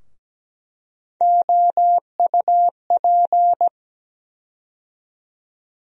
Morse code, heard as a keyed tone, about 17 wpm, 700 Hz. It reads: OUP
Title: USA Call Sign Postfixes - 3 Letters 17wpm